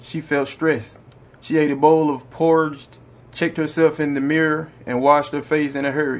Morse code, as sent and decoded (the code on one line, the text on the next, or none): none